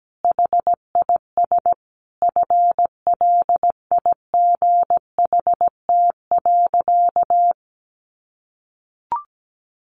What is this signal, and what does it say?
Morse code, heard as a keyed tone, about 17 wpm, 700 Hz. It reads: HIS FLIGHT.
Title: Sentences from Top 1400 Words 17wpm